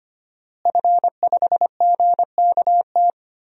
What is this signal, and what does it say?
Morse code, heard as a keyed tone, about 25 wpm, 700 Hz. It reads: F5GKT
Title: DXCC-100 Call Signs - 5 Characters - Repeated 3x - 25wpm